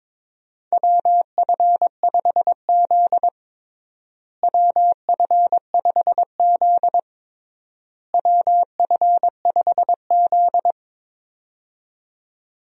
Morse code, heard as a keyed tone, about 22 wpm, 700 Hz. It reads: WF5Z WF5Z WF5Z
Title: USA Call Signs - 4 Characters - Repeated 3x - 22wpm